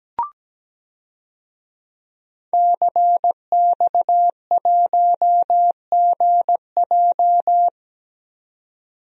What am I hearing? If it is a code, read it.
Morse code, heard as a keyed tone, about 17 wpm, 700 Hz. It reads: CX1GJ
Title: DXCC-100 Call Signs - 5 Characters - 17wpm